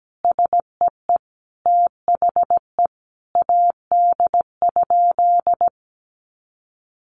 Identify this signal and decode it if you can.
Morse code, heard as a keyed tone, about 17 wpm, 700 Hz. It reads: SEE THE AD?
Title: Sentences from Top 1300 Words 17wpm